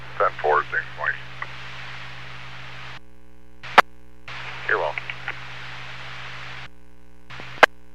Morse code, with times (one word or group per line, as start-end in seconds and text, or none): none